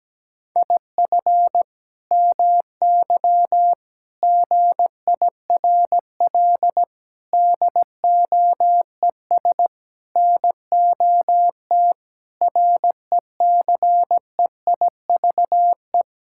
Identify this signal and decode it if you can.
Morse code, heard as a keyed tone, about 17 wpm, 700 Hz. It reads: IF MY GIRL DOES NOT RECEIVE
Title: Sentences from Top 1600 Words 17wpm